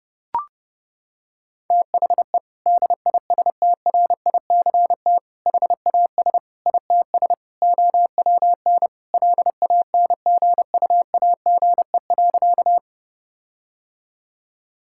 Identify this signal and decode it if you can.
Morse code, heard as a keyed tone, about 30 wpm, 700 Hz. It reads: THE DISTRICT HAS ITS OWN LANGUAGE.
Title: Sentences from Top 1600 Words 30wpm